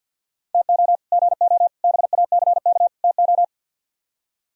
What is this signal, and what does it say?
Morse code, heard as a keyed tone, about 64 wpm, 700 Hz. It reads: TO GO BACK TO